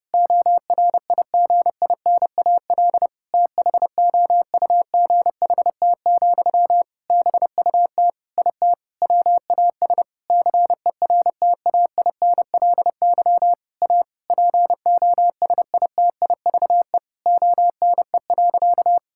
Morse code, heard as a keyed tone, about 32 wpm, 700 Hz. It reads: ORIGINAL THOUGHT, BUT IT WAS CERTAINLY A POSITIVE ONE.
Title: Sentences from Top 1000 Words - Review Part 6 - 30wpm